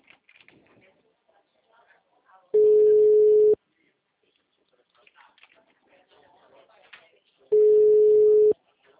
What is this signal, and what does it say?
Ringback tone (Europe, 425 Hz), with speech.
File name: out- -2002- - - .43936